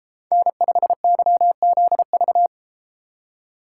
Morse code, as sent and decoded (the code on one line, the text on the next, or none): -. ..... -.-- --.. ...-
N5YZV